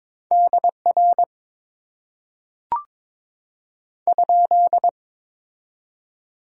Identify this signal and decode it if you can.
Morse code, heard as a keyed tone, about 22 wpm, 700 Hz. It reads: DR ?
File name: Top 90 QSO Elements - 6x Character Spacing - 22wpm